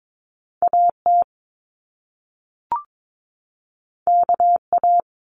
Morse code, heard as a keyed tone, about 22 wpm, 700 Hz. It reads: AT KA
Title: Top Two Letter Combinations in Words 22wpm